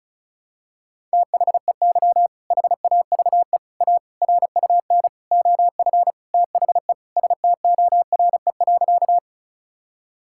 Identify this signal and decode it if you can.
Morse code, heard as a keyed tone, about 34 wpm, 700 Hz. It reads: THEY HAVE A RUN OF THE STORE.
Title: Sentences from Top 1000 Words - Review Part 5 - 35wpm